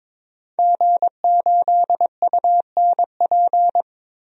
Morse code, heard as a keyed tone, about 21 wpm, 700 Hz. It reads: G8UNP
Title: DXCC-100 Call Signs - 5 Characters - 22wpm